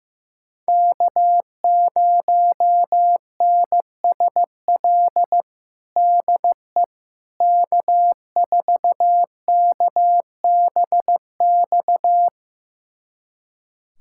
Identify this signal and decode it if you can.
Morse code, heard as a keyed tone, about 15 wpm, 700 Hz. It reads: K0NSL DE K4KBX
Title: USA Callsign DE Callsign - 5 Characters - 15wpm